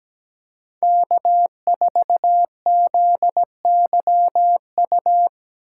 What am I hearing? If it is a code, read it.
Morse code, heard as a keyed tone, about 17 wpm, 700 Hz. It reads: K 4 Z Y U
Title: USA Call Signs - 5 Characters - 8x Character Spacing - 17wpm